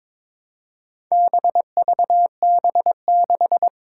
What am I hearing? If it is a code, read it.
Morse code, heard as a keyed tone, about 22 wpm, 700 Hz. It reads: BVB6
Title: Four Challenging Letter-Number Combinations 22wpm